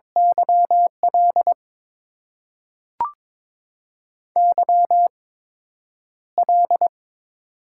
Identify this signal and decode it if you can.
Morse code, heard as a keyed tone, about 22 wpm, spaced as for 9 wpm, 690 Hz. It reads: YL YL
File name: Top 90 QSO Elements - 8x Character Spacing - 22wpm